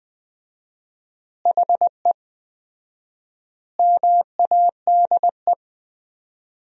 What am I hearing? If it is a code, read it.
Morse code, heard as a keyed tone, about 20 wpm, 700 Hz. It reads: HE MADE
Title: 4x Word-Spacing - Sentences from Top 300 Words - 20wpm